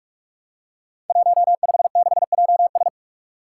Morse code, heard as a keyed tone, about 44 wpm, 700 Hz. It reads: 156JS
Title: Five Challenging Letter-Number Combinations 45wpm